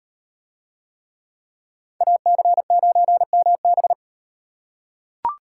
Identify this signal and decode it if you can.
Morse code, heard as a keyed tone, about 38 wpm, 700 Hz. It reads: AC9MB